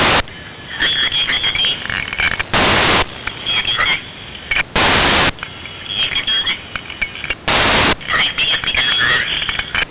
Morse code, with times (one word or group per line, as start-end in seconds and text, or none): none